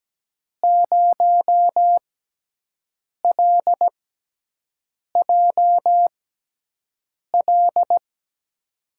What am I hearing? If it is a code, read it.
Morse code, heard as a keyed tone, about 17 wpm, spaced as for 6 wpm, 700 Hz. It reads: 0LJL
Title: USA Call Signs - 6 Characters - 6x Character Spacing - 17wpm